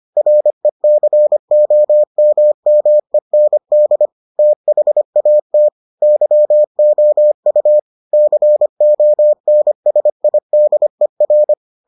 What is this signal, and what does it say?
Morse code, heard as a keyed tone, about 25 wpm, 600 Hz. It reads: RECOMMEND THAT YOU CONSIDER